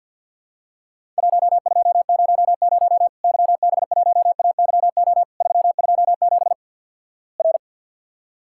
Morse code, heard as a keyed tone, about 50 wpm, 700 Hz. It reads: 1200 YB1AYO 327 R
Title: Worked All Europe DX Contest - QTC Exchange with Repeats - 50wpm